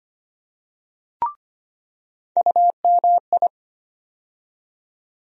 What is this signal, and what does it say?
Morse code, heard as a keyed tone, about 25 wpm, 700 Hz. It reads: UMI
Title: USA Call Sign Postfixes - 3 Letters 25wpm